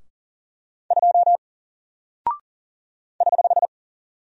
Morse code, heard as a keyed tone, about 40 wpm, 700 Hz. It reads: 2 <HH>
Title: Single Character (Letter:Number:Punctuation:Prosign) 40wpm